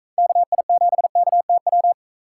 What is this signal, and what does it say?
Morse code, heard as a keyed tone, about 45 wpm, 700 Hz. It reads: KI7KTW